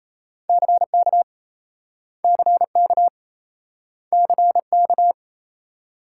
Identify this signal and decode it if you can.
Morse code, heard as a keyed tone, about 38 wpm, 700 Hz. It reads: CK CK CK